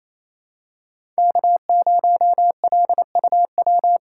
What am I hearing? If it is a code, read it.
Morse code, heard as a keyed tone, about 28 wpm, 700 Hz. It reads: K0LUW
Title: K1USN SST QSO Practice - 28wpm